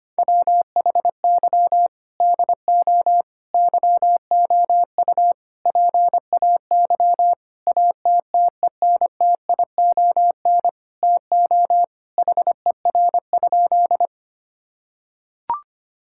Morse code, heard as a keyed tone, about 25 wpm, 700 Hz. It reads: WHY DO YOU PAY ATTENTION TO HER?
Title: Sentences from Top 1000 Words - Review Part 6 - 25wpm